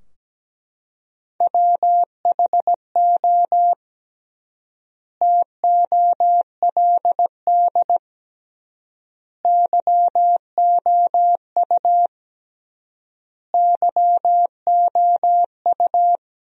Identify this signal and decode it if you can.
Morse code, heard as a keyed tone, about 17 wpm, 700 Hz. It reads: WHO TOLD YOU YOU
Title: 3x Word-Spacing - Sentences from Top 400 Words - 17wpm